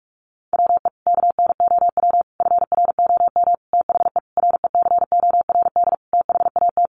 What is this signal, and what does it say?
Morse code, heard as a keyed tone, about 44 wpm, 700 Hz. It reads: WE KNOW FROM THE RECORD THAT
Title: Sentences from Top 500 Words 45wpm